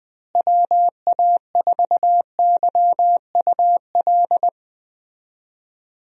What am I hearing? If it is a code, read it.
Morse code, heard as a keyed tone, about 20 wpm, 700 Hz. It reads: WA4YUL
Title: USA Call Signs - 6 Characters - Repeated 3x - 20wpm